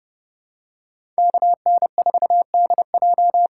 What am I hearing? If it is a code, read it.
Morse code, heard as a keyed tone, about 32 wpm, 700 Hz. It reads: KN4DJ
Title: USA Call Signs - 6 Characters - 6x Character Spacing - 30wpm